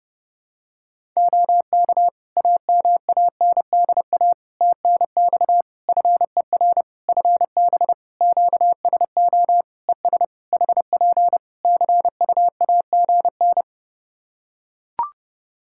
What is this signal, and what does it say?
Morse code, heard as a keyed tone, about 30 wpm, 700 Hz. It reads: OK AMANDA TNX FER FB QSO ES HP CUAGN
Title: Practice QSO - Exchange 4 - Ending - 30wpm